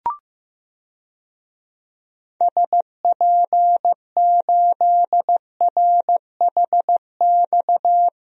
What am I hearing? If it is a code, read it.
Morse code, heard as a keyed tone, about 15 wpm, 700 Hz. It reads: SP8RHX
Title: DXCC-100 Call Signs - 6 Characters - 15wpm